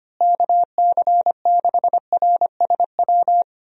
Morse code, heard as a keyed tone, about 25 wpm, 700 Hz. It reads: KC6RSW
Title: USA Call Signs - 6 Characters - Repeated 3x - 25wpm